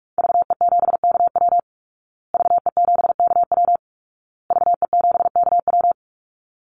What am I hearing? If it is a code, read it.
Morse code, heard as a keyed tone, about 44 wpm, 700 Hz. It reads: VE7KW VE7KW VE7KW
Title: Call Signs Repeated 2x - 45wpm